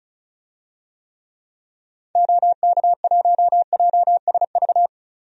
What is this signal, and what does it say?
Morse code, heard as a keyed tone, about 34 wpm, 700 Hz. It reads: OK1JSV
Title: DXCC-100 Call Signs - 6 Characters - Repeated 2x - 35wpm